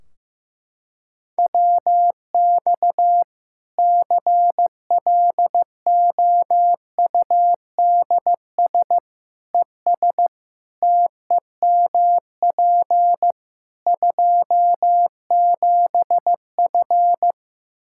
Morse code, heard as a keyed tone, about 15 wpm, 700 Hz. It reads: WX CLOUDS ES TEMP 27F
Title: Practice QSO - Exchange 2 - Info - 15wpm